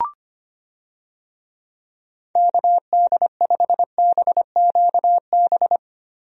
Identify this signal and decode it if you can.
Morse code, heard as a keyed tone, about 25 wpm, 700 Hz. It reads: KD5BQB
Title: USA Call Signs - 6 Characters - Repeated 2x - 25wpm